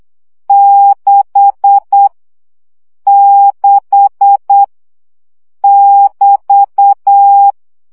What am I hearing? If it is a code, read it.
Morse code, heard as a keyed tone, about 8 wpm, 800 Hz. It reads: 6 6 <BT>
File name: CW gyakorlas 6 7 baud